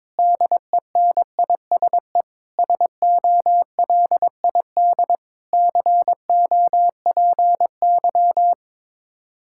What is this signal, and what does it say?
Morse code, heard as a keyed tone, about 22 wpm, 700 Hz. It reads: DENISE SOLID COPY
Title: K1USN SST QSO Practice 22wpm